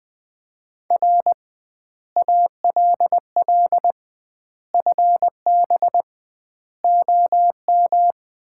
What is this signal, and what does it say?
Morse code, heard as a keyed tone, about 20 wpm, 700 Hz. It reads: R ALL FB OM
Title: Common QSO Phrases - 2x Word-Spacing - 20wpm